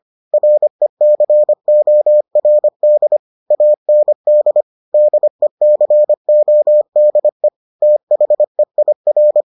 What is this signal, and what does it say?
Morse code, heard as a keyed tone, about 25 wpm, 600 Hz. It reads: RECORD AND DECODE THEIR